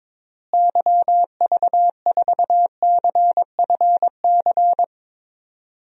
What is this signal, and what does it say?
Morse code, heard as a keyed tone, about 21 wpm, 700 Hz. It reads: YV4CFC
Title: DXCC-100 Call Signs - 6 Characters - Repeated 2x - 22wpm